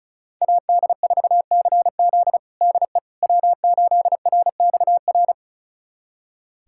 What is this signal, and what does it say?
Morse code, heard as a keyed tone, about 34 wpm, 700 Hz. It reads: AD4CZ DE W8RXR
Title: USA Callsign DE Callsign - 5 Characters - 35wpm